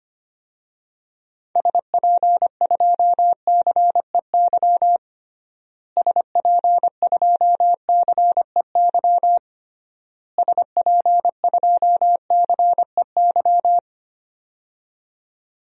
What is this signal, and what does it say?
Morse code, heard as a keyed tone, about 25 wpm, 700 Hz. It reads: SP2CEY SP2CEY SP2CEY
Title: DXCC-100 Call Signs - 6 Characters - Repeated 3x - 25wpm